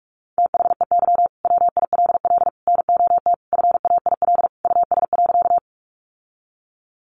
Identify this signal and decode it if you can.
Morse code, heard as a keyed tone, about 44 wpm, 700 Hz. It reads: THEY WILL NOT FAIL US.
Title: Sentences from Top 1100 Words 45wpm